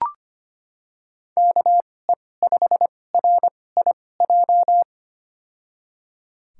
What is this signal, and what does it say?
Morse code, heard as a keyed tone, about 25 wpm, spaced as for 18 wpm, 700 Hz. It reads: KE5RIJ
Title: USA Call Signs - 6 Characters - 2x Character Spacing - 25wpm